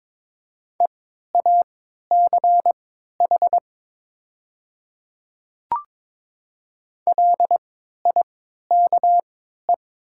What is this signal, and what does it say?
Morse code, heard as a keyed tone, about 22 wpm, spaced as for 12 wpm, 700 Hz. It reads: EACH LIKE
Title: 3x Character Spacing - Top 200 Words - 22wpm